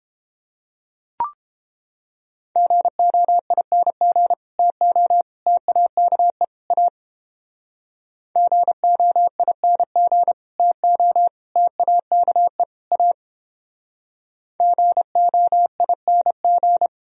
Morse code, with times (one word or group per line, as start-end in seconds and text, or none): 2.6-4.3 GOING
4.6-5.2 TO
5.5-6.4 TAKE
6.7-6.9 A
8.4-10.3 GOING
10.6-11.3 TO
11.6-12.6 TAKE
12.9-13.1 A
14.6-16.9 GOING